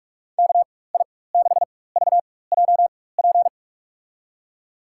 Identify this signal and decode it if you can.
Morse code, heard as a keyed tone, about 42 wpm, spaced as for 21 wpm, 700 Hz. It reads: KI6VJP
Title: USA Call Signs - 6 Characters - 4x Character Spacing - 45wpm